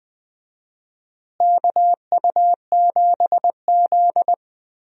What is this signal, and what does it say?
Morse code, heard as a keyed tone, about 20 wpm, 700 Hz. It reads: KU7Z
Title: USA Call Signs - 4 Characters - Repeated 3x - 20wpm